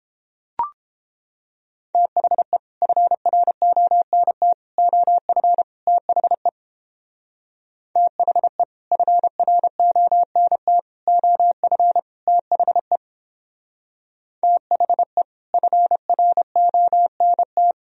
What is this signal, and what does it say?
Morse code, heard as a keyed tone, about 29 wpm, 700 Hz. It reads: THE FRONT OF THE THE FRONT OF THE THE FRONT